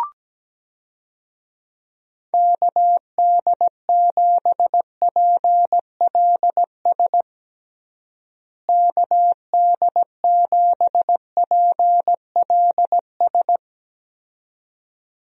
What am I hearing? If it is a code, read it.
Morse code, heard as a keyed tone, about 17 wpm, 700 Hz. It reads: KD7PLS KD7PLS
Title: USA Call Signs - 6 Characters - Repeated 2x - 17wpm